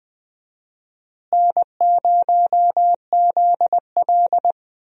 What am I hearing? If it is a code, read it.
Morse code, heard as a keyed tone, about 20 wpm, 700 Hz. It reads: N0ZL
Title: USA Call Signs - 4 Characters - Repeated 2x - 20wpm